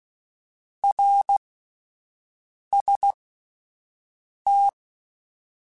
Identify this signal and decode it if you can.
Morse code, heard as a keyed tone, about 16 wpm, spaced as for 5 wpm, 800 Hz. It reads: RST
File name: (spaced) / QSOs Spaced-mp3s / WA7QOX 2 @10 sp